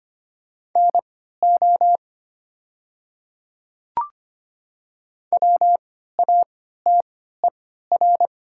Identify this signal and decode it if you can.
Morse code, heard as a keyed tone, about 25 wpm, spaced as for 14 wpm, 700 Hz. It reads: NO WATER
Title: 3x Character Spacing - Top 100 Words - 25wpm